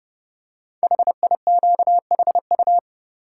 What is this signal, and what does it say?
Morse code, heard as a keyed tone, about 30 wpm, 700 Hz. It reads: HIQHU
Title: Code Groups - 5 Letters - 30wpm